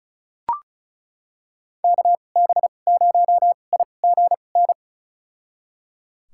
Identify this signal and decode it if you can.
Morse code, heard as a keyed tone, about 34 wpm, spaced as for 25 wpm, 700 Hz. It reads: KB0IGN
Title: USA Call Signs - 6 Characters - 2x Character Spacing - 35wpm